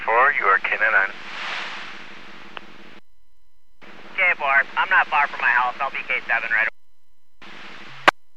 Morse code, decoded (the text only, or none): none